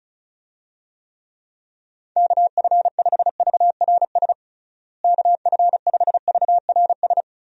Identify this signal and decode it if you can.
Morse code, heard as a keyed tone, about 36 wpm, 700 Hz. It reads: KF5VRS KF5VRS
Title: USA Call Signs - 6 Characters - Repeated 2x - 35wpm